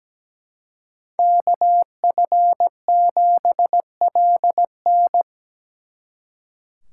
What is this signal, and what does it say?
Morse code, heard as a keyed tone, about 20 wpm, 700 Hz. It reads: KF7LN
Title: Speed Racing - USA 2x2 Call Signs 17wpm